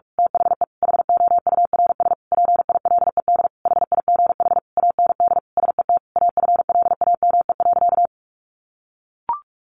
Morse code, heard as a keyed tone, about 44 wpm, 700 Hz. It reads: THE HOURS PILED HIGH AND SET AFLAME.
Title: A Sound Of Thunder 45wpm